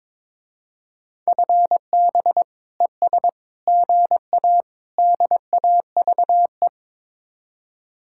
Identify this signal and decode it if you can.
Morse code, heard as a keyed tone, about 22 wpm, 700 Hz. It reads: FB ES GA DAVE
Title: Speed-Racing Common QSO Phrases Encore - 22wpm